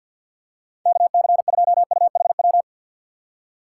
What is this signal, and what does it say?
Morse code, heard as a keyed tone, about 51 wpm, 700 Hz. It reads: KC2UHW
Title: USA Call Signs - 6 Characters - 4x Character Spacing - 50wpm